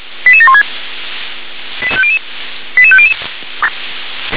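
Five-tone selective call (ZVEI), heard as quick runs of five tones, tones of about 70 ms. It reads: 88716 88500
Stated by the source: Five-tone ZVEI selective call received by SDR.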